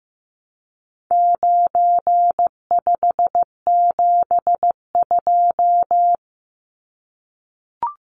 Morse code, heard as a keyed tone, about 15 wpm, 700 Hz. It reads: 9572
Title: Numbers - 4-Digit 15wpm